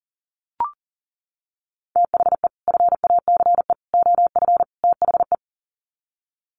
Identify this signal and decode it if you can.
Morse code, heard as a keyed tone, about 40 wpm, 700 Hz. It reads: THE FACE OF THE
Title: Sets of 4 Words 40wpm